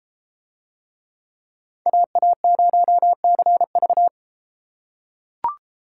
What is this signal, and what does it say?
Morse code, heard as a keyed tone, about 33 wpm, 700 Hz. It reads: AA0CV